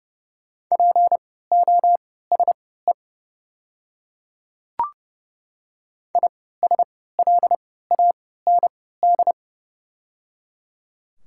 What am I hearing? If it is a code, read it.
Morse code, heard as a keyed tone, about 30 wpm, spaced as for 17 wpm, 700 Hz. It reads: POSE ISLAND
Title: 3x Character Spacing - Top 500 Words - 30wpm